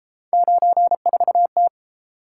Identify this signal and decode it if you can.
Morse code, heard as a keyed tone, about 31 wpm, 700 Hz. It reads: 94T